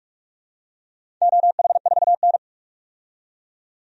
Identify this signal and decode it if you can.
Morse code, heard as a keyed tone, about 43 wpm, 700 Hz. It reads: OH4N
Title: DXCC-100 Call Signs - 4 Characters - 45wpm